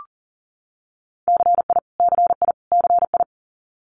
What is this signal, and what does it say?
Morse code, heard as a keyed tone, about 43 wpm, 700 Hz. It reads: CI CI CI
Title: Multi-letter Phonemes Repeated 3x 40wpm